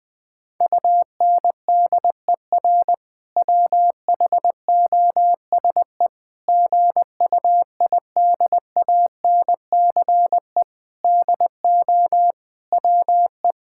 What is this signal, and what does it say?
Morse code, heard as a keyed tone, about 20 wpm, 700 Hz. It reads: UNDER WHOSE GUIDANCE DO WE
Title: Sentences from Top 1600 Words 20wpm